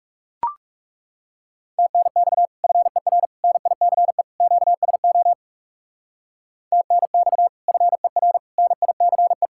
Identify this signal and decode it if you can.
Morse code, heard as a keyed tone, about 35 wpm, 700 Hz. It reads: TNX FER NICE QSO TNX FER NICE
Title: Speed-Racing Common QSO Phrases Encore - 30wpm